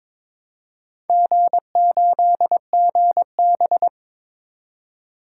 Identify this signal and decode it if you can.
Morse code, heard as a keyed tone, about 22 wpm, 700 Hz. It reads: G8GB
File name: Four Letter-Number Combinations - 22wpm